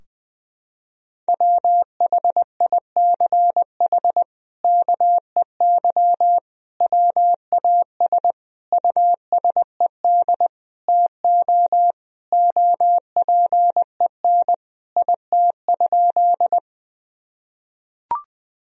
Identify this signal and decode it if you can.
Morse code, heard as a keyed tone, about 20 wpm, 700 Hz. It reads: WHICH KEY WAS USED TO OPEN IT?
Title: Sentences from Top 1000 Words - Review Part 4 - 20wpm